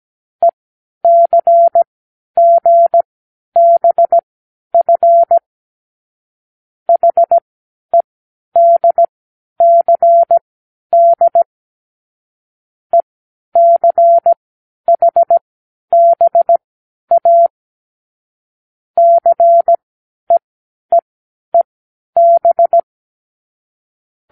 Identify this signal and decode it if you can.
Morse code, heard as a keyed tone, about 17 wpm, spaced as for 10 wpm, 680 Hz. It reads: ECGBF HEDCD ECHBA CEEEB
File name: k7qo / Lesson12(A-H)